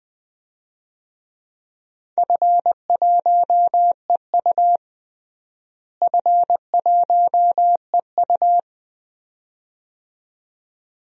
Morse code, heard as a keyed tone, about 20 wpm, 700 Hz. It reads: F1EU F1EU
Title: DXCC-100 Call Signs - 4 Characters - Repeated 2x - 20wpm